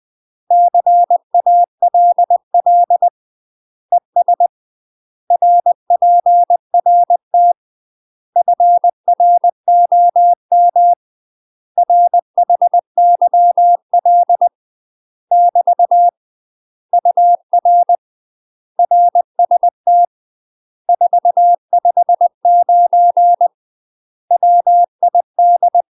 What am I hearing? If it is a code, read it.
Morse code, heard as a keyed tone, about 20 wpm, 700 Hz. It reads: CALL ES RPRT FROM RHYL <BT> UR RST 459 WID
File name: UK MASTER QSOs TRAINING / 20WPM / UK MASTER QSOs 0008